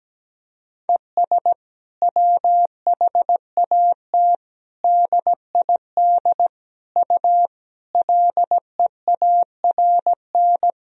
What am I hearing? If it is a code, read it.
Morse code, heard as a keyed tone, about 17 wpm, 700 Hz. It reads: ES WHAT DID U LEARN
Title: Sentences from Top 300 Words and Abbreviations 17wpm